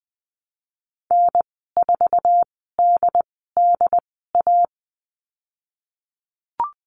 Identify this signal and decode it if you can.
Morse code, heard as a keyed tone, about 20 wpm, spaced as for 14 wpm, 700 Hz. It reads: N4DDA
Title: 082 Lesson 14g - Number 2 - Call Signs 20wpm